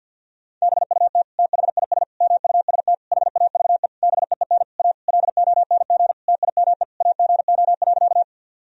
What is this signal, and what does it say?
Morse code, heard as a keyed tone, about 50 wpm, 700 Hz. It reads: BUT THIS MUST HAVE BEEN A LONG TIME AGO.
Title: Sentences from Top 1000 Words - Review Part 3 - 50wpm